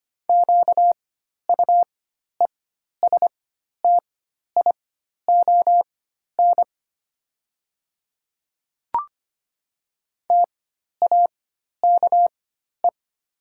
Morse code, heard as a keyed tone, about 25 wpm, spaced as for 12 wpm, 690 Hz. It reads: QUESTION TAKE
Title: 4x Character Spacing - Top 500 Words - 25wpm